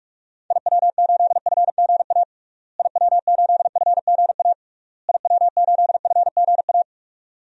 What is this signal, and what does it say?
Morse code, heard as a keyed tone, about 46 wpm, 700 Hz. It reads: IW8FGA IW8FGA IW8FGA
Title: DXCC-100 Call Signs - 6 Characters - Repeated 3x - 45wpm